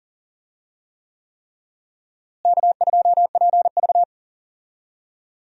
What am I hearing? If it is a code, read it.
Morse code, heard as a keyed tone, about 40 wpm, 700 Hz. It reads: K2PV
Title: USA Call Signs - 4 Characters - Repeated 3x - 40wpm